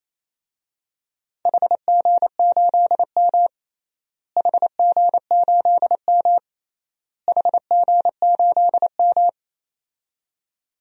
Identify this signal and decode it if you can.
Morse code, heard as a keyed tone, about 28 wpm, 700 Hz. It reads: HG8M HG8M HG8M
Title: DXCC-100 Call Signs - 4 Characters - Repeated 3x - 28wpm